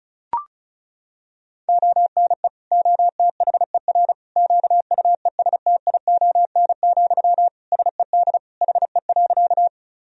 Morse code, heard as a keyed tone, about 36 wpm, 700 Hz. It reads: ONE OTHER QUESTION, SED HE.
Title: Sentences from Top 400 Words and Abbreviations 35wpm